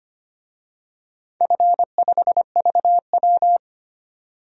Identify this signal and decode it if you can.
Morse code, heard as a keyed tone, about 25 wpm, 700 Hz. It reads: F5VW
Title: DXCC-100 Call Signs - 4 Characters - 25wpm